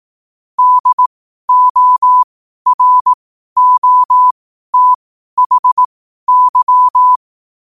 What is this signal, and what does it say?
Morse code, heard as a keyed tone, about 18 wpm, spaced as for 12 wpm, 1000 Hz. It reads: DOROTHY
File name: qotdfiles / 12wpm